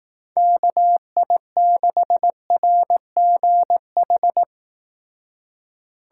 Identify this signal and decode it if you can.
Morse code, heard as a keyed tone, about 18 wpm, 700 Hz. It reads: KI6RGH
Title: Speed Racing - USA 2x3 Call Signs 15wpm